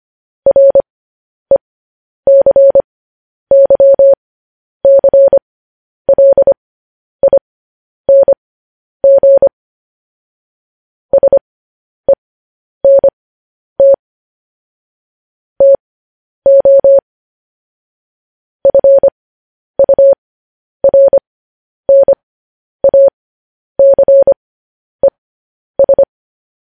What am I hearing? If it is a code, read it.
Morse code, heard as a keyed tone, about 25 wpm, spaced as for 10 wpm, 550 Hz. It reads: RECYCLING SENT TO FURNACES
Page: news-headlines-in-morse-code-at-10-wpm